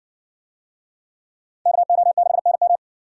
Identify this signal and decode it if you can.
Morse code, heard as a keyed tone, about 55 wpm, 700 Hz. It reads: KO6NM